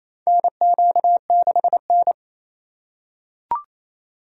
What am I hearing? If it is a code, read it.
Morse code, heard as a keyed tone, about 28 wpm, 700 Hz. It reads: NQ6N
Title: USA Call Signs - 4 Characters - 8x Character Spacing - 28wpm